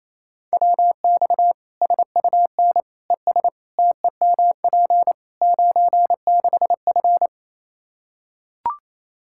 Morse code, heard as a keyed tone, about 28 wpm, 700 Hz. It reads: WX SUN ES TEMP 96F
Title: Practice QSO - Exchange 2 - Info - 28wpm